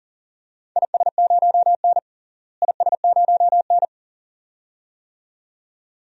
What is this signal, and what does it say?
Morse code, heard as a keyed tone, about 40 wpm, 700 Hz. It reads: IS0N IS0N
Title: DXCC-100 Call Signs - 4 Characters - Repeated 2x - 40wpm